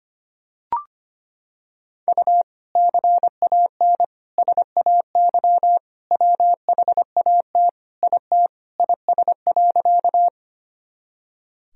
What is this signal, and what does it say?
Morse code, heard as a keyed tone, about 25 wpm, 700 Hz. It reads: U CAN SAY WHAT IT IS.
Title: Sentences from Top 200 Words and Abbreviations 25wpm